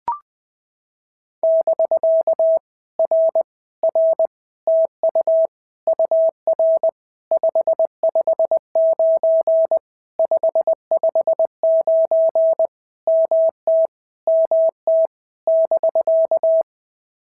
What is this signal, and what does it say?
Morse code, heard as a keyed tone, about 20 wpm, 650 Hz. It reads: <BK> R R TU UR 559 559 MT MT <BK>
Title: POTA QSO Practice - 20wpm